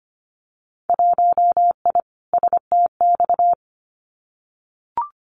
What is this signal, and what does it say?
Morse code, heard as a keyed tone, about 25 wpm, 700 Hz. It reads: 1I STX
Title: Winter Field Day Exchanges 25wpm